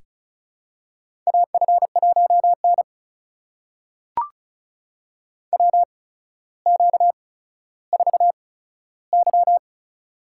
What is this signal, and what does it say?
Morse code, heard as a keyed tone, about 36 wpm, spaced as for 15 wpm, 700 Hz. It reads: AF1N WQ4Y
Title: USA Call Signs - 4 Characters - 8x Character Spacing - 35wpm